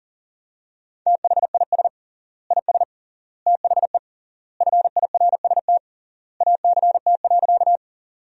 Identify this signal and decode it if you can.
Morse code, heard as a keyed tone, about 40 wpm, 700 Hz. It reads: THIS IS THE FIRST ACT.
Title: 3x Word-Spacing - Sentences from Top 200 Words - 40wpm